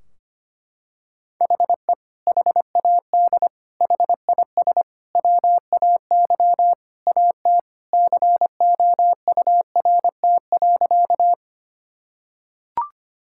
Morse code, heard as a keyed tone, about 25 wpm, 700 Hz. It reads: HE HAD HIS WAY AT COURT.
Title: Sentences from Top 1600 Words 25wpm